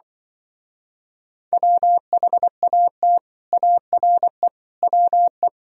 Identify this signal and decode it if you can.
Morse code, heard as a keyed tone, about 24 wpm, 700 Hz. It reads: WHAT ARE WE
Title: Speed-Racing - Sets of 5 Words 20wpm